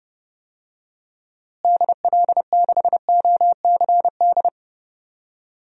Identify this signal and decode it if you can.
Morse code, heard as a keyed tone, about 32 wpm, 700 Hz. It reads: DL6OCD
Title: DXCC-100 Call Signs - 6 Characters - Repeated 3x - 30wpm